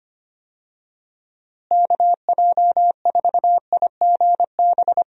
Morse code, heard as a keyed tone, about 25 wpm, 700 Hz. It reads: KJ4IGB
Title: USA Call Signs - 6 Characters - Repeated 3x - 25wpm